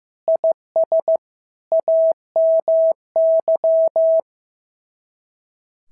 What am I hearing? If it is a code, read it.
Morse code, heard as a keyed tone, about 14 wpm, 650 Hz. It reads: IS AMY
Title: SOTA QSO Practice 15wpm